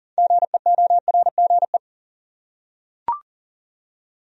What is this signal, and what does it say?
Morse code, heard as a keyed tone, about 38 wpm, 700 Hz. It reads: GEORGE
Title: USA Common Names - 40wpm